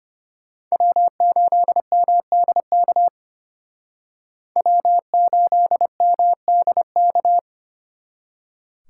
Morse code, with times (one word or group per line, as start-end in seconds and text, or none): 0.7-3.1 W8MDK
4.6-7.4 W8MDK